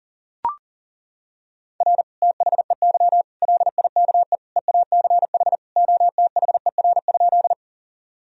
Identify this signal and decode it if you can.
Morse code, heard as a keyed tone, about 40 wpm, 700 Hz. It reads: R THEY LIKE EACH OTHER?
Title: Sentences from Top 200 Words and Abbreviations 40wpm